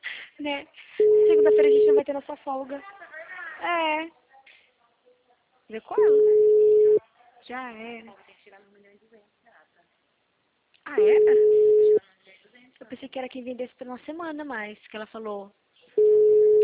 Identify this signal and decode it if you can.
Ringback tone (Europe, 425 Hz), with speech.